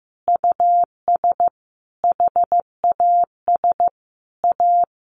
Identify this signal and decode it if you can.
Morse code, heard as a keyed tone, about 15 wpm, 700 Hz. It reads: US HAS A
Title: Sentences from Top 1500 Words 15wpm